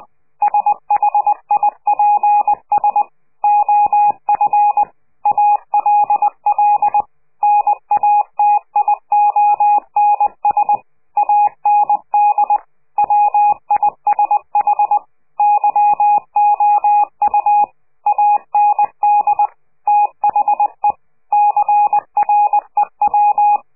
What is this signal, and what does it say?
Morse code, heard as a keyed tone, about 20 wpm, 830 Hz. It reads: SHIPS OF ALL NATIONS AND WISH YOU AND THE CREW
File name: CQ de PCH laatste bericht farewell